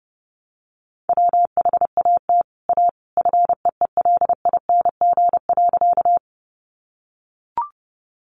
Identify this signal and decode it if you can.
Morse code, heard as a keyed tone, about 30 wpm, 700 Hz. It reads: WHAT A FEELING.
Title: Sentences from Top 1200 Words 30wpm